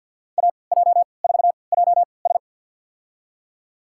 Morse code, heard as a keyed tone, about 51 wpm, spaced as for 29 wpm, 700 Hz. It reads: AJ4JS
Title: USA Call Signs - 5 Characters - 3x Character Spacing - 50wpm